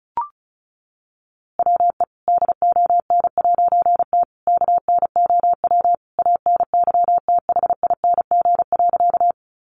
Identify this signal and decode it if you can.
Morse code, heard as a keyed tone, about 34 wpm, 700 Hz. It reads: WE DON'T KNOW ANYTHING.
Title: A Sound Of Thunder 35wpm